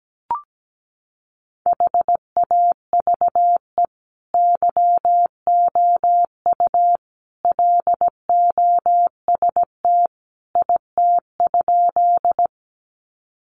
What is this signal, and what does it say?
Morse code, heard as a keyed tone, about 17 wpm, 700 Hz. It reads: HAVE YOU LOST IT?
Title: Sentences from Top 1400 Words 17wpm